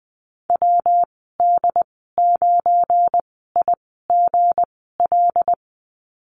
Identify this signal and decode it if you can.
Morse code, heard as a keyed tone, about 20 wpm, spaced as for 15 wpm, 700 Hz. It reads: WD9IGL
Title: 173 Lesson 28g - Number 9 - Call Signs 20wpm